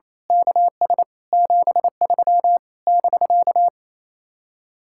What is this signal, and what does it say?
Morse code, heard as a keyed tone, about 26 wpm, 700 Hz. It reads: KS 73 <BK>
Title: Speed-Racing Common QSO Phrases - 28wpm